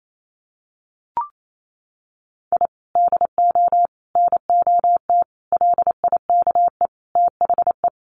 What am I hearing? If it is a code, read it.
Morse code, heard as a keyed tone, about 28 wpm, 700 Hz. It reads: I DO NOT LIKE THE
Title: Sets of 5 Words Encore 28wpm